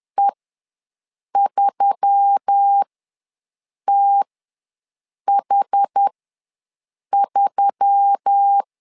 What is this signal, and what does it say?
Morse code, heard as a keyed tone, about 10 wpm, spaced as for 6 wpm, 800 Hz. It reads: E3TH3
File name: LEZIONI / 4 Q1S3 ws 10wpm 2437